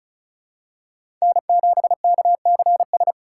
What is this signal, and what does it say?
Morse code, heard as a keyed tone, about 36 wpm, 700 Hz. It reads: N7KCS
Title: USA Call Signs - 5 Characters - 3x Character Spacing - 35wpm